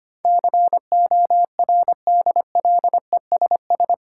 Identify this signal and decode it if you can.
Morse code, heard as a keyed tone, about 25 wpm, 690 Hz. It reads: CORDLESS
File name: practice / 25wpm / Difficult Words with Hints - 25wpm